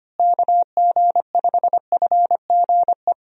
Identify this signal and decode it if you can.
Morse code, heard as a keyed tone, about 25 wpm, 700 Hz. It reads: KG5FGE
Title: USA Call Signs - 6 Characters - Repeated 2x - 25wpm